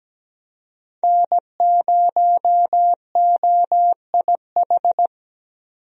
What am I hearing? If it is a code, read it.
Morse code, heard as a keyed tone, about 17 wpm, 700 Hz. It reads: N0OIH
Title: USA Call Signs - 5 Characters - 17wpm